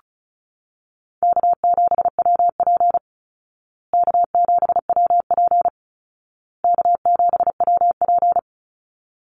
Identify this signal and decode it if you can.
Morse code, heard as a keyed tone, about 34 wpm, 700 Hz. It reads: K7WP K7WP K7WP
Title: Easy Call Signs Repeated 3x - 35wpm